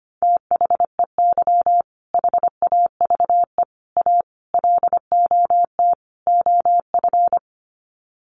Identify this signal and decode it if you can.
Morse code, heard as a keyed tone, about 25 wpm, 700 Hz. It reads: THEY HAVE A LOT OF
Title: Sets of 5 Words Encore2 25wpm